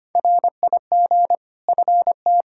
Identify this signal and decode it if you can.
Morse code, heard as a keyed tone, about 25 wpm, 700 Hz. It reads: RIG FT
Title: Practice QSO - Exchange 2 - Info - 25wpm